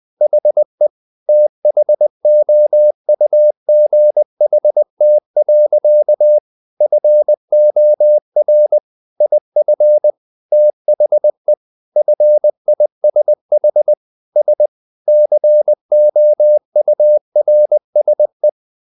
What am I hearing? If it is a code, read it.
Morse code, heard as a keyed tone, about 20 wpm, 600 Hz. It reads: HE THOUGHT. FOR IF THE FISH S COURSE
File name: the old man and the sea / the old man and the sea part 1 20wpm